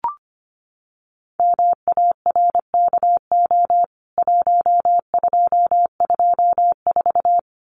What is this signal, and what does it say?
Morse code, heard as a keyed tone, about 25 wpm, 700 Hz. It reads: MARKO 1224
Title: CWT Exchange 25wpm